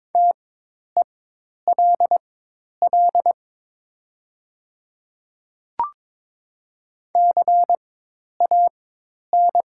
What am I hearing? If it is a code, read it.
Morse code, heard as a keyed tone, about 22 wpm, spaced as for 10 wpm, 700 Hz. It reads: TELL CAN
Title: 4x Character Spacing - Top 500 Words - 22wpm